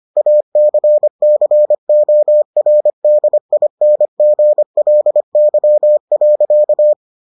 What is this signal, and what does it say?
Morse code, heard as a keyed tone, about 25 wpm, 600 Hz. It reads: ACCORDINGLY.